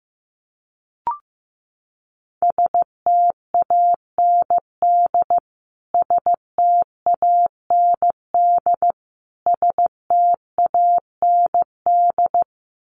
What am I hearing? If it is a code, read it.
Morse code, heard as a keyed tone, about 15 wpm, 700 Hz. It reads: STAND STAND STAND
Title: Top 500 Words Repeated 15wpm